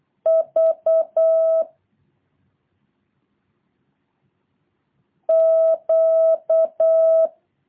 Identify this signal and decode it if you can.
Morse code, heard as a keyed tone, about 8 wpm, 650 Hz. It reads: V Q
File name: jh8cbh / 2023cw